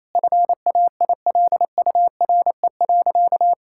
Morse code, heard as a keyed tone, about 28 wpm, 700 Hz. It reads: FAILURE.
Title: Sentences from Top 1000 Words - Review Part 6 - 28wpm